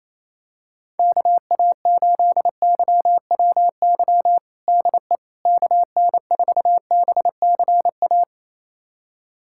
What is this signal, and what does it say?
Morse code, heard as a keyed tone, about 28 wpm, 700 Hz. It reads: KA8YWY DE KN4BCA
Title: USA Callsign DE Callsign - 6 Characters - 28wpm